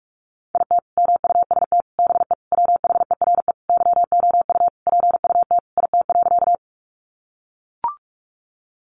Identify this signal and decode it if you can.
Morse code, heard as a keyed tone, about 44 wpm, 700 Hz. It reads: IT MUST BE WHERE YOU PUT IT.
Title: Sentences from Top 200 Words 45wpm